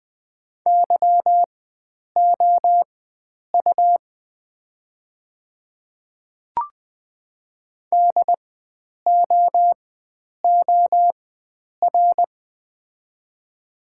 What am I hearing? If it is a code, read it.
Morse code, heard as a keyed tone, about 20 wpm, spaced as for 9 wpm, 700 Hz. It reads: YOU DOOR
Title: 4x Character Spacing - Top 300 Words - 20wpm